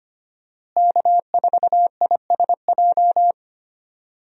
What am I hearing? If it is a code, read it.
Morse code, heard as a keyed tone, about 25 wpm, 700 Hz. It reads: K4ISJ
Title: USA Call Signs - 5 Characters - Repeated 2x - 25wpm